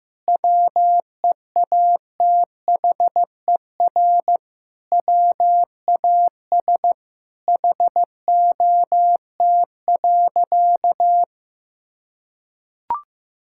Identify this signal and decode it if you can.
Morse code, heard as a keyed tone, about 15 wpm, 700 Hz. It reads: WEATHER WAS HOT.
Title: Sentences from Top 1000 Words - Review Part 5 - 15wpm